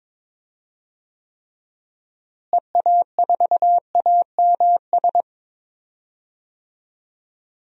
Morse code, heard as a keyed tone, about 22 wpm, 700 Hz. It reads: EA4AMS
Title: DXCC-100 Call Signs - 6 Characters - Repeated 3x - 22wpm